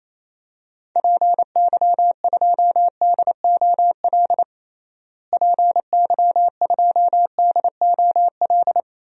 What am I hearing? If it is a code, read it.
Morse code, heard as a keyed tone, about 27 wpm, 700 Hz. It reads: PY2DOL PY2DOL
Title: DXCC-100 Call Signs - 6 Characters - Repeated 2x - 28wpm